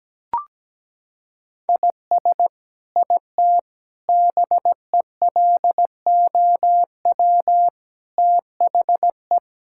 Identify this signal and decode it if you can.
Morse code, heard as a keyed tone, about 17 wpm, 700 Hz. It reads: IS IT BELOW THE
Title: Sentences from Top 1000 Words - Review Part 5 - 17wpm